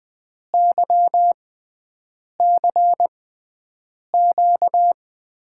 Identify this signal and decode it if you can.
Morse code, heard as a keyed tone, about 21 wpm, spaced as for 7 wpm, 700 Hz. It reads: YCQ
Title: USA Call Signs - 6 Characters - 6x Character Spacing - 20wpm